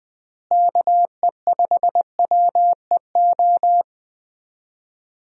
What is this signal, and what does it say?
Morse code, heard as a keyed tone, about 20 wpm, 700 Hz. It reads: KE5WEO
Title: USA Call Signs - 6 Characters - Repeated 2x - 20wpm